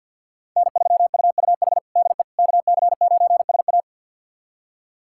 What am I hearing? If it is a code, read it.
Morse code, heard as a keyed tone, about 52 wpm, 700 Hz. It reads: N3UUH DE KC9SA
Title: USA Callsign DE Callsign - 5 Characters - 50wpm